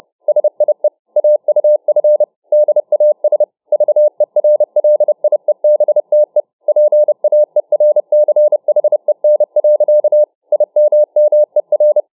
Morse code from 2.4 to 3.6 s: DAS